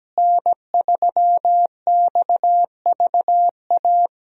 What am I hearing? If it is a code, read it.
Morse code, heard as a keyed tone, about 17 wpm, 700 Hz. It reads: N3XVA
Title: USA Call Signs - 5 Characters - Repeated 3x - 17wpm